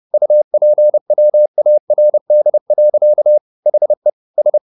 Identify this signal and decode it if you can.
Morse code, heard as a keyed tone, about 30 wpm, 600 Hz. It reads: UPWARD. HE S
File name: the old man and the sea / the old man and the sea 1 30wpm